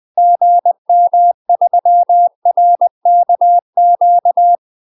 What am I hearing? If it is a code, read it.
Morse code, heard as a keyed tone, about 20 wpm, 700 Hz. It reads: GM3RKQ
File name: UK MASTER QSOs TRAINING / 20WPM / UK MASTER QSOs 0029